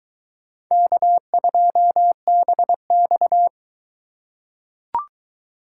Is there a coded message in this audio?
Yes: Morse code from 0.7 to 3.5 s, heard as a keyed tone, about 22 wpm, 700 Hz. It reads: K2BX